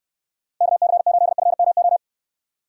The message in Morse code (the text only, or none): KC8FMO